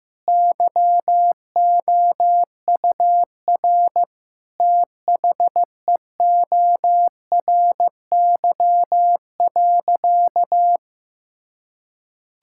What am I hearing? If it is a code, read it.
Morse code, heard as a keyed tone, about 15 wpm, 700 Hz. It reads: YOUR THEORY.
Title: Sentences from Top 1000 Words - Review Part 5 - 15wpm